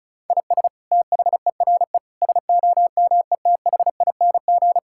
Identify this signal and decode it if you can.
Morse code, heard as a keyed tone, about 36 wpm, 700 Hz. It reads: IS THERE SOMETHING
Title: Sentences from Top 1000 Words - Review Part 6 - 35wpm